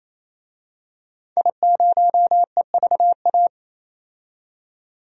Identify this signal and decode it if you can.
Morse code, heard as a keyed tone, about 28 wpm, 700 Hz. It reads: I0EVA
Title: DXCC-100 Call Signs - 5 Characters - 28wpm